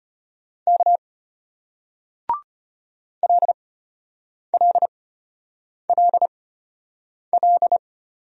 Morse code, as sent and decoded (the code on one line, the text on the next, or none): -.- / .-.. .-.. .-.. .-..
K LLLL